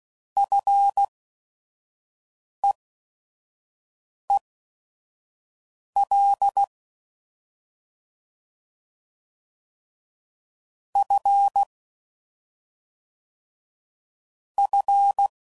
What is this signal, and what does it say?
Morse code, heard as a keyed tone, about 16 wpm, spaced as for 5 wpm, 800 Hz. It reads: FEEL F F